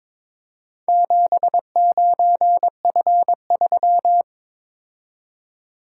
Morse code, heard as a keyed tone, about 22 wpm, 700 Hz. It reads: 79F3
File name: practice / 22wpm / Four Letter-Number Combinations - 22wpm